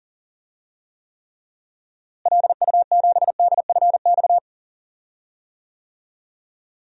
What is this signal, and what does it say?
Morse code, heard as a keyed tone, about 37 wpm, 700 Hz. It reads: LU7DFX
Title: DXCC-100 Call Signs - 6 Characters - Repeated 2x - 40wpm